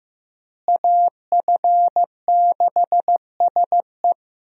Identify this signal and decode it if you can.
Morse code, heard as a keyed tone, about 15 wpm, 700 Hz. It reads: AF6SE
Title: USA Call Signs - 5 Characters - Repeated 3x - 15wpm